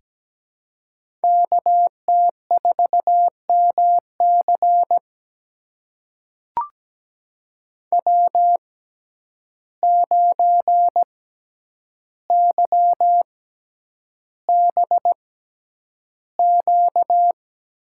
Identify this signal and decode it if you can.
Morse code, heard as a keyed tone, about 17 wpm, spaced as for 9 wpm, 700 Hz. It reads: KT4MC W9YBQ
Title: USA Call Signs - 5 Characters - 6x Character Spacing - 17wpm